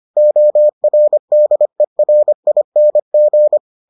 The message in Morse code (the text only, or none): ORDERING